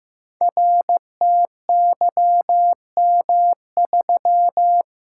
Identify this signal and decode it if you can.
Morse code, heard as a keyed tone, about 15 wpm, 700 Hz. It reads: RTYM3
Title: Code Groups - 5 Letter:Numbers - 15wpm